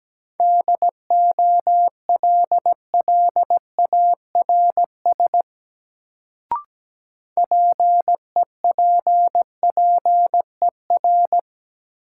Morse code, heard as a keyed tone, about 17 wpm, 700 Hz. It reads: DOLLARS PEPPER
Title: Sets of 5 Words - New Word Drill - 17wpm